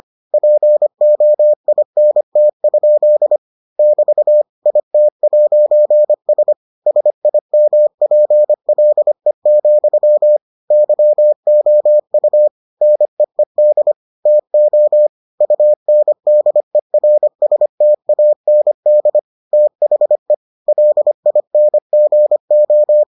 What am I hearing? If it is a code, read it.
Morse code, heard as a keyed tone, about 25 wpm, 600 Hz. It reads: POINT? <BT> IT'S SIMPLE, YOU NEED TO UNDERSTAND THE LINGO